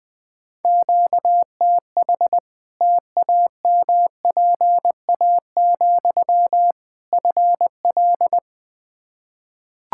Morse code, heard as a keyed tone, about 20 wpm, 700 Hz. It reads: QTH TAMPA, FL
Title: Practice QSO - Exchange 1 - Beginning - 20wpm